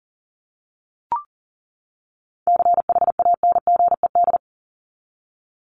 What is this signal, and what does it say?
Morse code, heard as a keyed tone, about 40 wpm, 700 Hz. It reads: CHANGED
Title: ED Suffix - Top 100 Words with Base Words 40wpm